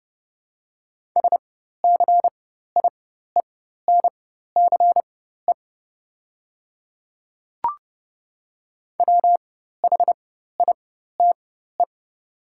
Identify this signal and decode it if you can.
Morse code, heard as a keyed tone, about 32 wpm, spaced as for 14 wpm, 700 Hz. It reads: SCIENCE WHITE
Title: 4x Character Spacing - Top 500 Words - 30wpm